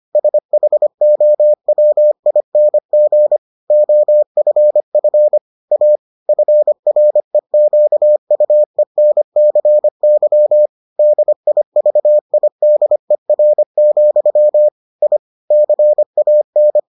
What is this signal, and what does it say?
Morse code, heard as a keyed tone, about 25 wpm, 600 Hz. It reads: SHOWING OFF A FREQUENCY DIVIDER, I CAN